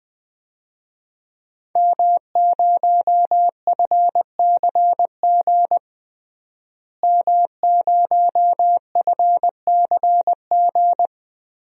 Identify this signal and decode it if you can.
Morse code, heard as a keyed tone, about 20 wpm, 700 Hz. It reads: M0FCG M0FCG
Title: DXCC-100 Call Signs - 5 Characters - Repeated 2x - 20wpm